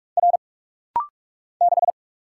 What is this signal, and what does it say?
Morse code, heard as a keyed tone, about 46 wpm, 700 Hz. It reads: R 6
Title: Single Letter-Number - Rapid-Fire 45wpm